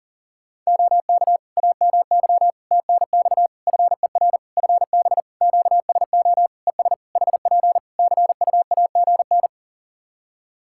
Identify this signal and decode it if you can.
Morse code, heard as a keyed tone, about 37 wpm, 700 Hz. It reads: OK AMY TNX FER FB QSO ES HP CUAGN
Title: Practice QSO - Exchange 4 - Ending - 40wpm